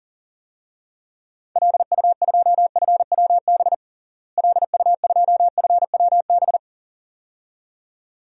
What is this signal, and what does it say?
Morse code, heard as a keyed tone, about 37 wpm, 700 Hz. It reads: LU2FWB LU2FWB
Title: DXCC-100 Call Signs - 6 Characters - Repeated 2x - 40wpm